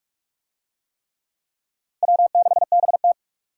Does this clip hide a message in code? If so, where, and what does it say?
Yes: Morse code from 2.0 to 3.1 s, heard as a keyed tone, about 43 wpm, 700 Hz. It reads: W6BT